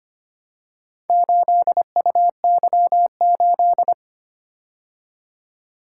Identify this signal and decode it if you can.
Morse code, heard as a keyed tone, about 25 wpm, 700 Hz. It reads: 8UY8
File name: practice / 25wpm / Four Letter-Number Combinations - 25wpm